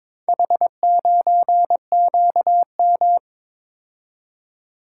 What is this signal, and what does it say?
Morse code, heard as a keyed tone, about 22 wpm, 700 Hz. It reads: H9QM
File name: practice / 22wpm / Four Letter-Number Combinations - 22wpm